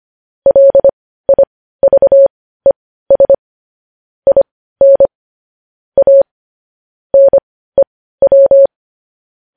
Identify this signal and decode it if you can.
Morse code, heard as a keyed tone, about 25 wpm, spaced as for 15 wpm, 550 Hz. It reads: LIVES IN A NEW
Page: news-headlines-in-morse-code-at-15-wpm